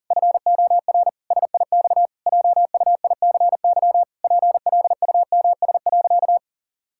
Morse code, heard as a keyed tone, about 40 wpm, 700 Hz. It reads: FOR SIX JUICY PLUMS.
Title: Pangram Challenge v2 - 40wpm